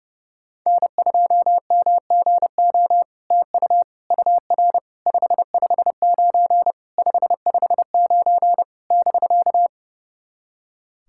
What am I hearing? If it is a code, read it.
Morse code, heard as a keyed tone, about 32 wpm, 700 Hz. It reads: N2MGO TU UR 559 559 <BK>
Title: SOTA QSO Practice 30wpm